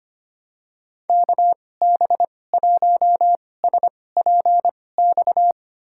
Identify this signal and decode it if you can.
Morse code, heard as a keyed tone, about 25 wpm, spaced as for 18 wpm, 700 Hz. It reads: KB1SPX
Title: USA Call Signs - 2x Character Spacing - 25wpm